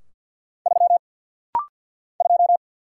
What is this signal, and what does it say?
Morse code, heard as a keyed tone, about 50 wpm, 700 Hz. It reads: 3 2
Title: Single Letter-Number - Rapid-Fire 50wpm